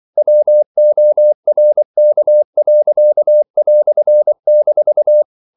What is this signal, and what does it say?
Morse code, heard as a keyed tone, about 24 wpm, 600 Hz. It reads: WORK." -
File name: CW / Practice / EnglishQuotations0224